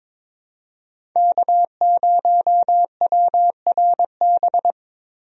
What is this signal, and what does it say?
Morse code, heard as a keyed tone, about 22 wpm, 700 Hz. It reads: K0WRB
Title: USA Call Signs - 5 Characters - 22wpm